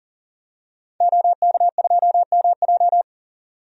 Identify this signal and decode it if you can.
Morse code, heard as a keyed tone, about 40 wpm, 700 Hz. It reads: OK2MJ
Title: DXCC-100 Call Signs - 5 Characters - 40wpm